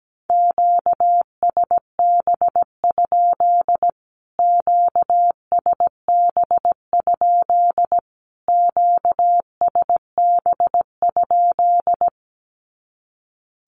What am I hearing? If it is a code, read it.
Morse code, heard as a keyed tone, about 17 wpm, 700 Hz. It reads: QSB? QSB? QSB?
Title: Q-Codes Repeated 3x 17wpm